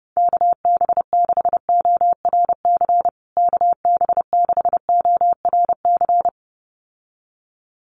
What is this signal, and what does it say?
Morse code, heard as a keyed tone, about 30 wpm, 700 Hz. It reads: KB6ORC KB6ORC
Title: USA Call Signs Repeated 2x 30wpm